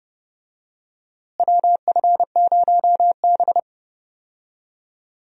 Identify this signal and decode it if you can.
Morse code, heard as a keyed tone, about 30 wpm, 700 Hz. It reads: WF0B
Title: USA Call Signs - 4 Characters - Repeated 3x - 30wpm